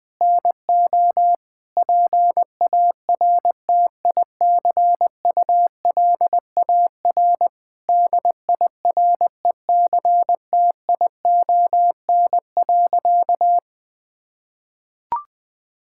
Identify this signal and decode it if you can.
Morse code, heard as a keyed tone, about 20 wpm, 700 Hz. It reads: NO PARTICULAR DIRECTION.
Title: Sentences from Top 1000 Words - Review Part 5 - 20wpm